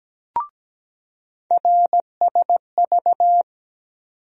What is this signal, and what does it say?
Morse code, heard as a keyed tone, about 17 wpm, 700 Hz. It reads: RSV
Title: USA Call Sign Postfixes - 3 Letters 17wpm